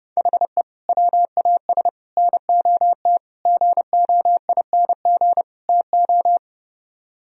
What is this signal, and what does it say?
Morse code, heard as a keyed tone, about 30 wpm, 700 Hz. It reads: HE WAS NOT GOING TO